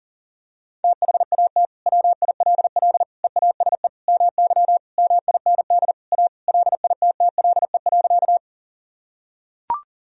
Morse code, heard as a keyed tone, about 40 wpm, 700 Hz. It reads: THAT WILL EASE MY MIND A LITTLE.
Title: Sentences from Top 1000 Words - Review Part 1 - 40wpm